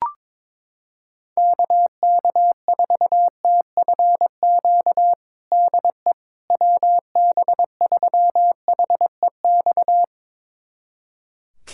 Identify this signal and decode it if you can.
Morse code, heard as a keyed tone, about 22 wpm, 700 Hz. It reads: KK4TFQ DE WB3HEX
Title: USA Callsign DE Callsign - 6 Characters - 22wpm